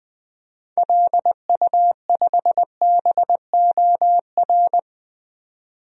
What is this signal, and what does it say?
Morse code, heard as a keyed tone, about 20 wpm, 700 Hz. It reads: LU5BOR
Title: DXCC-100 Call Signs - 6 Characters - 20wpm